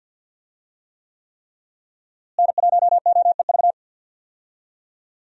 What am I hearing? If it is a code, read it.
Morse code, heard as a keyed tone, about 52 wpm, 700 Hz. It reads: N1OEV
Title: USA Call Signs - 5 Characters - Repeated 3x - 50wpm